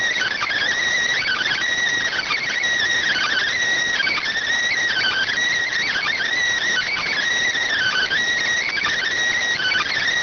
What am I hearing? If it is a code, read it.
Emergency Alert System (SAME) tones (header bursts), heard as harsh buzzing bursts. It is heard from 1.2 to 9.9 s.